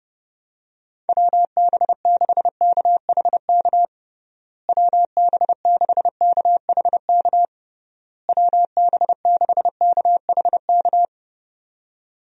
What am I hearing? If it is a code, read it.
Morse code, heard as a keyed tone, about 29 wpm, 700 Hz. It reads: WB6KHK WB6KHK WB6KHK
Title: USA Call Signs - 6 Characters - Repeated 3x - 30wpm